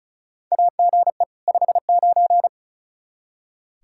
Morse code, heard as a keyed tone, about 36 wpm, 700 Hz. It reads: AGE 59
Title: Practice QSO - Exchange 3 - More Info - 35wpm